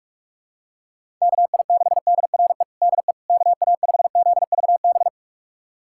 Morse code, heard as a keyed tone, about 41 wpm, 700 Hz. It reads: KI6DRE DE KA5ZVB
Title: USA Callsign DE Callsign 45wpm